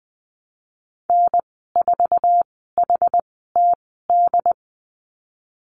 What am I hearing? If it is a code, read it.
Morse code, heard as a keyed tone, about 20 wpm, spaced as for 14 wpm, 700 Hz. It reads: N4HTD
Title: 069 Lesson 12g - Letter d - Call Signs 20wpm